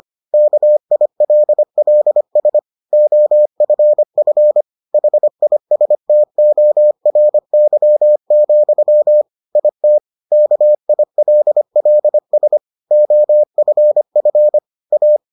Morse code, heard as a keyed tone, about 25 wpm, 600 Hz. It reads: KILLS OFF HISTORY, IT KILLS OFF A